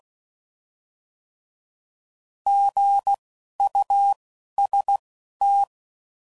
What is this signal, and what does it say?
Morse code, heard as a keyed tone, about 16 wpm, spaced as for 11 wpm, 800 Hz. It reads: GUST
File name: Prefixes / By First Letter